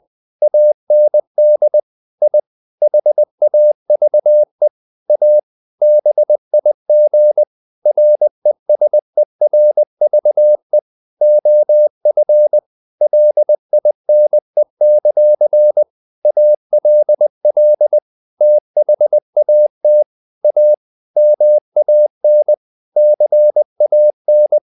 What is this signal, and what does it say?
Morse code, heard as a keyed tone, about 20 wpm, 600 Hz. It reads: AND I HAVE A BIG RESERVE OF LINE; ALL THAT A MAN CAN
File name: the old man and the sea / the old man and the sea part 1 20wpm